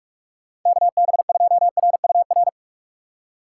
Morse code, heard as a keyed tone, about 46 wpm, 700 Hz. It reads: KB2RUR
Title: USA Call Signs - 3x Character Spacing - 45wpm